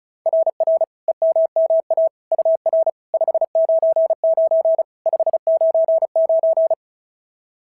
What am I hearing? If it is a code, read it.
Morse code, heard as a keyed tone, about 36 wpm, 690 Hz. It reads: RR EMMA UR 599 599
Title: SOTA QSO Practice 35wpm